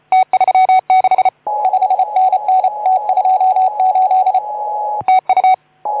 Morse code, heard as a keyed tone, about 32 wpm, 750 Hz. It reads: T36 5NN T28 TU
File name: Archiv / CUC181 / OK2ZC CUC 181